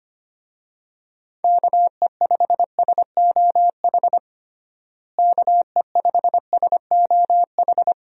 Morse code, heard as a keyed tone, about 25 wpm, 700 Hz. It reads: KE5SOH KE5SOH
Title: USA Call Signs - 6 Characters - Repeated 3x - 25wpm